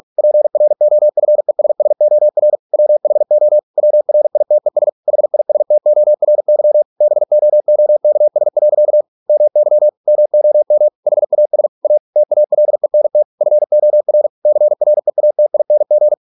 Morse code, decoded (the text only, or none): PROFESSOR WHO WRITES HISTORY BOOKS. MY MOM HAS A TALENT FOR CREATING